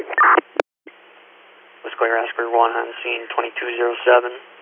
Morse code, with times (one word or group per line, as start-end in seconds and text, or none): none